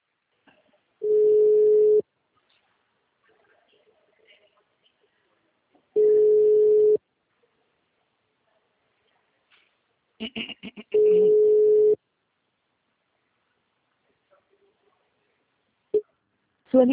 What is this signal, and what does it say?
Ringback tone (Europe, 425 Hz), with speech.